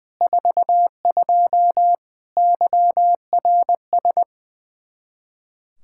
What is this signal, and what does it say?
Morse code, heard as a keyed tone, about 20 wpm, 700 Hz. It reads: 42 YRS
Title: Practice QSO - Exchange 3 - More Info - 20wpm